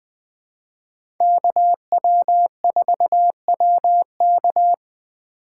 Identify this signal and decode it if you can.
Morse code, heard as a keyed tone, about 21 wpm, 700 Hz. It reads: KW4WK
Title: USA Call Signs - 5 Characters - 2x Character Spacing - 20wpm